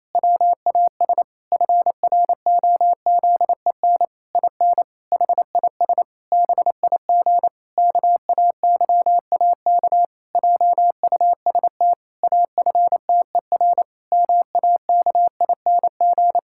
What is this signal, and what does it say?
Morse code, heard as a keyed tone, about 28 wpm, 700 Hz. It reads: WAS FROZEN IN HIS BIG KAYAK JUST AFTER MAKING
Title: Pangram Challenge v2 - 28wpm